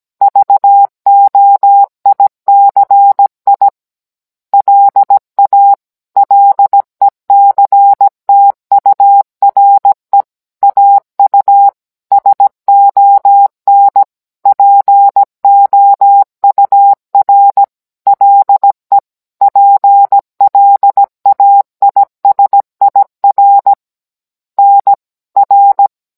Morse code, heard as a keyed tone, about 17 wpm, 800 Hz. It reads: VOICI LA LECTURE AU SON POUR LE PLAISIR NR